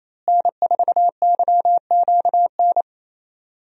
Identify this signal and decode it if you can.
Morse code, heard as a keyed tone, about 28 wpm, 700 Hz. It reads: N4YQN
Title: USA Call Signs - 5 Characters - Repeated 3x - 28wpm